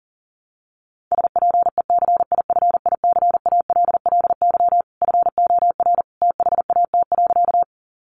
Morse code, heard as a keyed tone, about 40 wpm, 700 Hz. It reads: SPECIFICALLY FOR THAT.
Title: Sentences from Top 1100 Words 40wpm